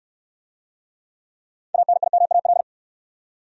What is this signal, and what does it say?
Morse code, heard as a keyed tone, about 61 wpm, 700 Hz. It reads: ANIMAL